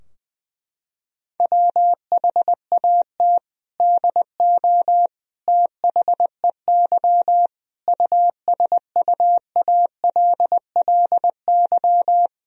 Morse code, heard as a keyed tone, about 20 wpm, 700 Hz. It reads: WHAT DO THEY USUALLY
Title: Sentences from Top 1000 Words - Review Part 3 - 20wpm